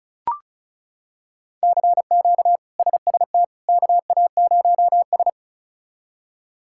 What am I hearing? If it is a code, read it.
Morse code, heard as a keyed tone, about 34 wpm, 700 Hz. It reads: CQ SST KA0S
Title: K1USN SST QSO Practice - 35wpm